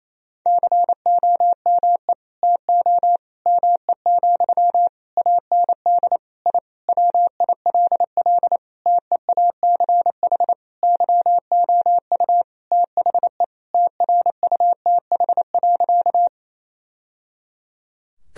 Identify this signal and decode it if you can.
Morse code, heard as a keyed tone, about 28 wpm, 700 Hz. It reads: COME TO ME, AND I WILL TEACH YOU THE TRUTH.
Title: Sentences from Top 1000 Words - Review Part 2 - 28wpm